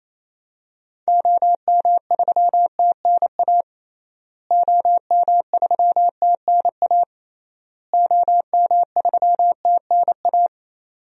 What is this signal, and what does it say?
Morse code, heard as a keyed tone, about 27 wpm, 700 Hz. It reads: OM3TNA OM3TNA OM3TNA
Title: DXCC-100 Call Signs - 6 Characters - Repeated 3x - 28wpm